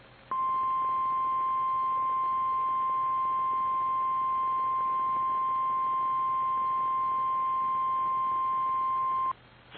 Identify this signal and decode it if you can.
Emergency Alert System (SAME) tones (the 1050 Hz attention tone). It is heard from 0.3 to 9.4 s.
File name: eas / KKPRKXPN / KKPRKXPN